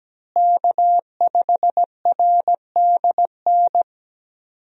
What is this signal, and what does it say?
Morse code, heard as a keyed tone, about 17 wpm, 700 Hz. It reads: K5RDN
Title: USA Call Signs - 5 Characters - Repeated 2x - 17wpm